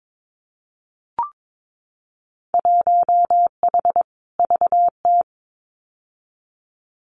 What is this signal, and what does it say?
Morse code, heard as a keyed tone, about 22 wpm, 700 Hz. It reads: 1H VT
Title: Winter Field Day Exchanges 22wpm